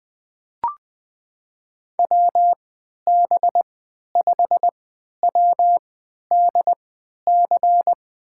Morse code, heard as a keyed tone, about 20 wpm, spaced as for 11 wpm, 700 Hz. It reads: WB5WDC
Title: USA Call Signs - 6 Characters - 3x Character Spacing - 20wpm